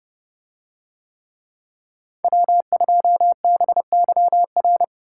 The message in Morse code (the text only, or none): W2BYR